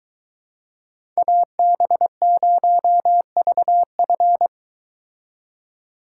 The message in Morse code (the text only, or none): AB0VF